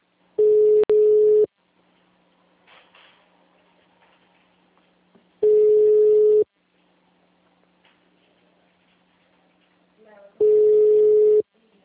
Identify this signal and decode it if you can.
Ringback tone (Europe, 425 Hz), with speech.
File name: out- -2002- - - .1744